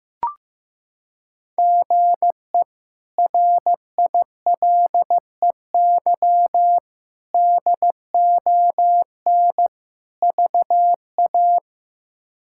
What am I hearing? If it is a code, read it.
Morse code, heard as a keyed tone, about 15 wpm, 700 Hz. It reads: GE RILEY DON VA
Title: K1USN SST QSO Practice - 15wpm